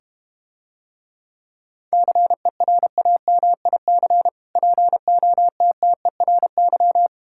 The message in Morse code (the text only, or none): CERAMIC POTTERY